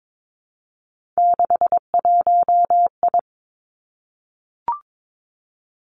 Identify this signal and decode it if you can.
Morse code, heard as a keyed tone, about 22 wpm, 700 Hz. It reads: 61I
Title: Three Challenging Letter-Number Combinations 22wpm